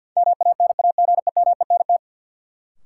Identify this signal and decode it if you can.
Morse code, heard as a keyed tone, about 49 wpm, 700 Hz. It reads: MANAGEMENT
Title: Sentences from Top 1000 Words - Review Part 4 - New Word Drill - 50wpm